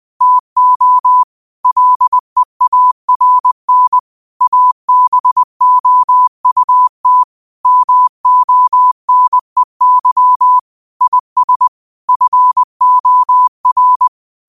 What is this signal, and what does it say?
Morse code, heard as a keyed tone, about 20 wpm, 1000 Hz. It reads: TO LEARN ABOUT MONEY IS FOR